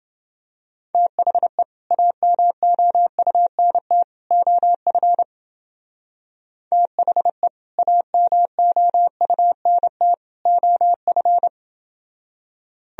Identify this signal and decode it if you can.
Morse code, heard as a keyed tone, about 30 wpm, 700 Hz. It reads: THE AMOUNT OF THE AMOUNT OF